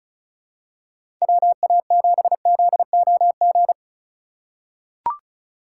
Morse code, heard as a keyed tone, about 36 wpm, 700 Hz. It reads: WA7ZOG
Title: USA Call Signs - 6 Characters - 2x Character Spacing - 35wpm